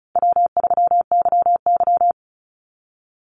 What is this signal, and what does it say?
Morse code, heard as a keyed tone, about 34 wpm, 700 Hz. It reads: W3YY
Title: Call Signs Repeated 2x - 35wpm